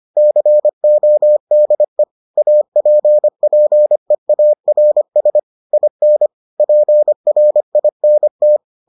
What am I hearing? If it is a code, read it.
Morse code, heard as a keyed tone, about 25 wpm, 600 Hz. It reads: CODE APPEARS IN PRINT